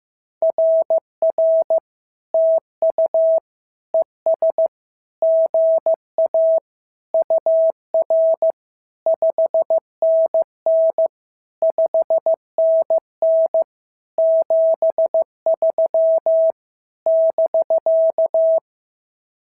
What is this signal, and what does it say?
Morse code, heard as a keyed tone, about 14 wpm, 650 Hz. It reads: RR TU ES GA UR 5NN 5NN 73 <BK>
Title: SOTA QSO Practice 15wpm